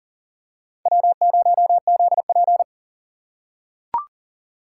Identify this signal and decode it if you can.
Morse code, heard as a keyed tone, about 40 wpm, 700 Hz. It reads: W0ZP
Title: K1USN SST QSO Practice - 40wpm